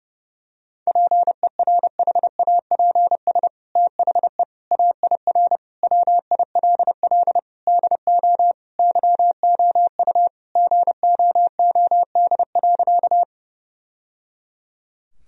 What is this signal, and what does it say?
Morse code, heard as a keyed tone, about 32 wpm, 700 Hz. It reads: PERHAPS THE AIR WILL DO YOU GOOD.
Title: Sentences from Top 1000 Words - Review Part 1 - 30wpm